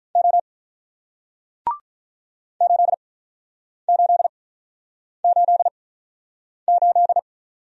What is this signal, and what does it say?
Morse code, heard as a keyed tone, about 48 wpm, spaced as for 9 wpm, 700 Hz. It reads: O 8888